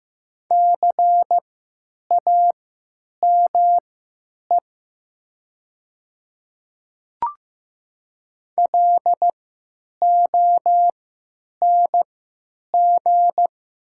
Morse code, heard as a keyed tone, about 15 wpm, spaced as for 9 wpm, 690 Hz. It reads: CAME LONG
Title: 3x Character Spacing - Top 300 Words - 15wpm